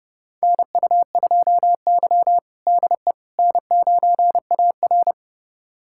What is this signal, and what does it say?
Morse code, heard as a keyed tone, about 30 wpm, 700 Hz. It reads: NU2Y DE N9AR
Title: USA Callsign DE Callsign - 4 Characters - 30wpm